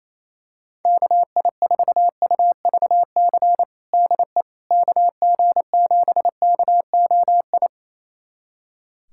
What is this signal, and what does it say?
Morse code, heard as a keyed tone, about 28 wpm, 700 Hz. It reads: KI4UVC DE KG7KOI
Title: USA Callsign DE Callsign - 6 Characters - 28wpm